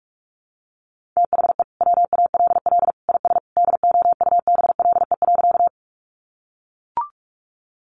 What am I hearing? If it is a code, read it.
Morse code, heard as a keyed tone, about 44 wpm, 700 Hz. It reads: THE WALL IS DOUBLE.
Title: Sentences from Top 1400 Words 45wpm